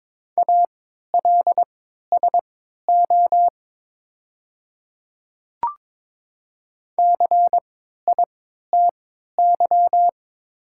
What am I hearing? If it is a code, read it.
Morse code, heard as a keyed tone, about 22 wpm, spaced as for 12 wpm, 700 Hz. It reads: ALSO CITY
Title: 3x Character Spacing - Top 500 Words - 22wpm